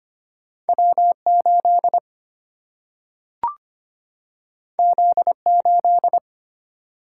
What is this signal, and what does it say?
Morse code, heard as a keyed tone, about 25 wpm, 700 Hz. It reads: W 8 Z 8
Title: Single-Letter Single-Number 25wpm